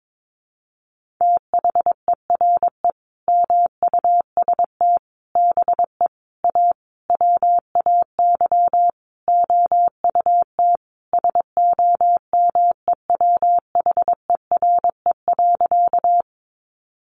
Morse code, heard as a keyed tone, about 22 wpm, 700 Hz. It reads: THERE MUST BE A WAY OUT SOMEWHERE.
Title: Sentences from Top 1400 Words 22wpm